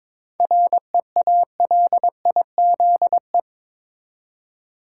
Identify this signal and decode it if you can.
Morse code, heard as a keyed tone, about 21 wpm, 700 Hz. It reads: REALIZE
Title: Seven Letter Words - 22wpm